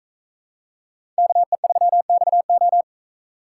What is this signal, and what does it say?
Morse code, heard as a keyed tone, about 40 wpm, 700 Hz. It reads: KE3XO